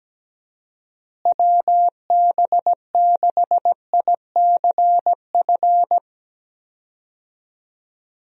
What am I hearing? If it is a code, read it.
Morse code, heard as a keyed tone, about 17 wpm, 700 Hz. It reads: WB6ICF
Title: USA Call Signs - 6 Characters - Repeated 2x - 17wpm